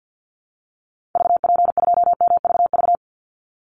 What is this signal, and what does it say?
Morse code, heard as a keyed tone, about 50 wpm, 700 Hz. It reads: VP2MVV
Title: Call Signs Repeated 2x - 50wpm